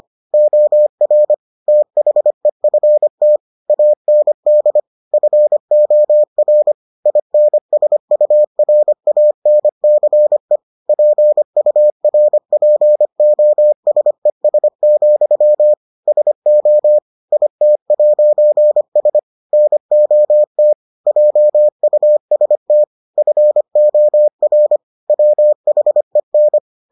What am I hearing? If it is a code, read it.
Morse code, heard as a keyed tone, about 25 wpm, 600 Hz. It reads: OR THEFT AND FOR INSURANCE PURPOSES, SO IT'S NOT JUST FOR WHEN